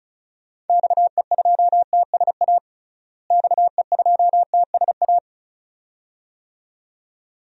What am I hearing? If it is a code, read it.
Morse code, heard as a keyed tone, about 36 wpm, 700 Hz. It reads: XE2TSA XE2TSA
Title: DXCC-100 Call Signs - 6 Characters - Repeated 2x - 35wpm